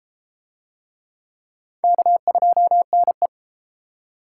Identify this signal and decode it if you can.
Morse code, heard as a keyed tone, about 32 wpm, 700 Hz. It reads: K2NE